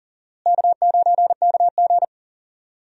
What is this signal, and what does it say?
Morse code, heard as a keyed tone, about 40 wpm, 700 Hz. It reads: K9KG
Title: USA Call Signs - 4 Characters - Repeated 3x - 40wpm